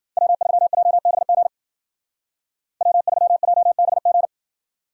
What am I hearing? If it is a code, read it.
Morse code, heard as a keyed tone, about 50 wpm, 700 Hz. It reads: W3JBG W3JBG
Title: Speed Racing - USA 1x3 Call Signs 40wpm